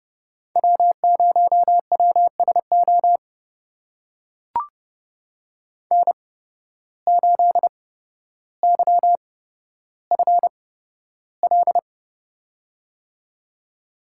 Morse code, heard as a keyed tone, about 29 wpm, spaced as for 13 wpm, 700 Hz. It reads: W0WSO N8YFL
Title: USA Call Signs - 5 Characters - 8x Character Spacing - 30wpm